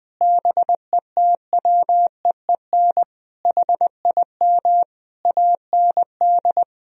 Morse code, heard as a keyed tone, about 20 wpm, 700 Hz. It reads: BETWEEN HIM AND
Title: Sentences from Top 1000 Words - Review Part 3 - 20wpm